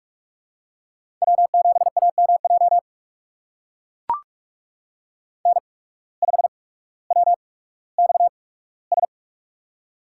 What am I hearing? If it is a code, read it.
Morse code, heard as a keyed tone, about 44 wpm, spaced as for 19 wpm, 690 Hz. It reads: W7AMJ N5WXS
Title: USA Call Signs - 5 Characters - 8x Character Spacing - 45wpm